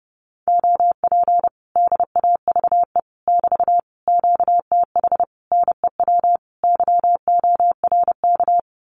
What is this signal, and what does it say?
Morse code, heard as a keyed tone, about 30 wpm, 700 Hz. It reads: OP DAVE <BT> QTH NEW YORK
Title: Common QSO Phrases Encore 30wpm